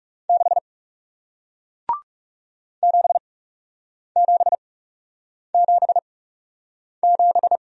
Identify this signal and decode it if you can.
Morse code, heard as a keyed tone, about 41 wpm, spaced as for 8 wpm, 700 Hz. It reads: 6 7777